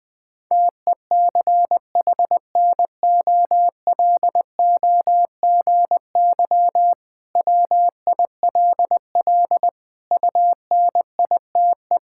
Morse code, heard as a keyed tone, about 20 wpm, 700 Hz. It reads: TECHNOLOGY WILL UNITE
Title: Sentences from Top 1000 Words - Review Part 5 - 20wpm